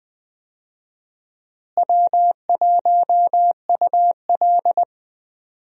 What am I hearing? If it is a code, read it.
Morse code, heard as a keyed tone, about 20 wpm, 700 Hz. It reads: W1UL
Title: USA Call Signs - 4 Characters - Repeated 2x - 20wpm